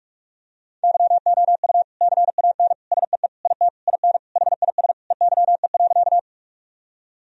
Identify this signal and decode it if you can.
Morse code, heard as a keyed tone, about 46 wpm, 700 Hz. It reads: YOU CAN SEE IT IN HIS EYE.
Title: Sentences from Top 1000 Words - Review Part 1 - 45wpm